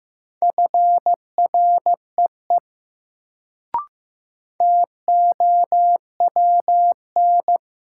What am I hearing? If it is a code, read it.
Morse code, heard as a keyed tone, about 15 wpm, 700 Hz. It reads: FREE TOWN
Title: Sentences from Top 500 Words - New Word Drill - 15wpm